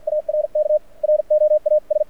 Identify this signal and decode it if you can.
Morse code, heard as a keyed tone, about 53 wpm, spaced as for 45 wpm, 600 Hz. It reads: ARK ROAS
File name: archived / CRSnet